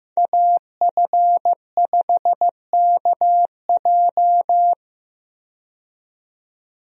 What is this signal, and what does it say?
Morse code, heard as a keyed tone, about 15 wpm, 700 Hz. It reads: AF5KJ
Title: USA Call Signs - 5 Characters - 15wpm